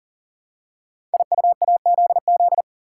Morse code, heard as a keyed tone, about 39 wpm, 700 Hz. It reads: IUAZZ
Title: Code Groups - 5 Letters - 40wpm